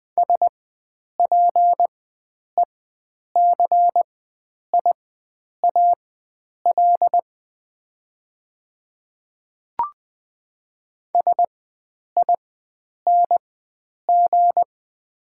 Morse code, heard as a keyed tone, about 21 wpm, spaced as for 9 wpm, 700 Hz. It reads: SPECIAL SING
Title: 4x Character Spacing - Top 500 Words - 20wpm